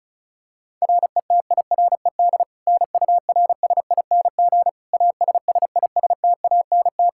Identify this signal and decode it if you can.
Morse code, heard as a keyed tone, about 34 wpm, 700 Hz. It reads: RETIRED NURSING ASSISTANT
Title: Practice QSO - Exchange 3 - More Info - 35wpm